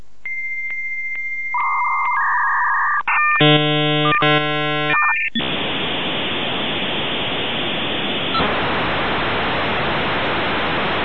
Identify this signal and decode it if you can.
Dial-up modem handshake.